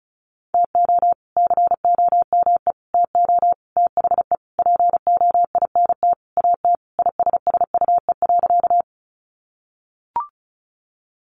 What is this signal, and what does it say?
Morse code, heard as a keyed tone, about 34 wpm, 700 Hz. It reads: TO COME TO THE POINT AT ISSUE.
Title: Sentences from Top 900 Words 35wpm